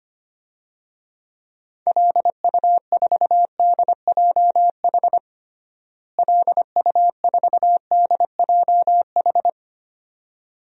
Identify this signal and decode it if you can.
Morse code, heard as a keyed tone, about 25 wpm, 700 Hz. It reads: LU4DJH LU4DJH
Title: DXCC-100 Call Signs - 6 Characters - Repeated 2x - 25wpm